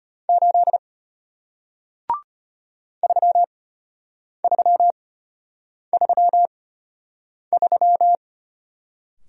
Morse code, heard as a keyed tone, about 34 wpm, spaced as for 8 wpm, 700 Hz. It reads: 8 3333